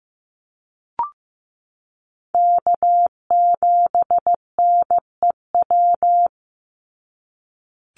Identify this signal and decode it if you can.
Morse code, heard as a keyed tone, about 15 wpm, 700 Hz. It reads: K7NEW
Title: NA SOTA Chaser Callsigns 15wpm